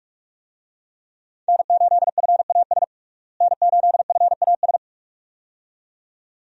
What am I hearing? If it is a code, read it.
Morse code, heard as a keyed tone, about 46 wpm, 700 Hz. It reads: N8FAS N8FAS
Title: USA Call Signs - 5 Characters - Repeated 2x - 45wpm